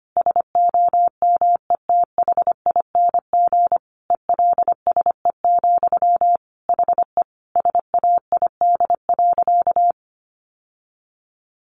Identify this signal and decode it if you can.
Morse code, heard as a keyed tone, about 25 wpm, 700 Hz. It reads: SOMETHING ELSE, HE SAID.
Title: Sentences from Top 800 Words 25wpm